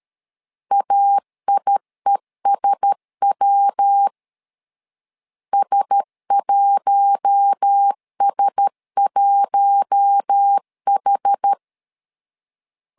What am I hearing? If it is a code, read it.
Morse code, heard as a keyed tone, about 13 wpm, 800 Hz. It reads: AIESW S1S1H
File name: LEZIONI / 4 Q1S 12wpm 1802